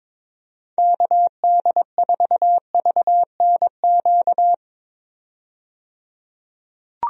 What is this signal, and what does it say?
Morse code, heard as a keyed tone, about 22 wpm, 700 Hz. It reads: KD4VNQ
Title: USA Call Signs - 6 Characters - 22wpm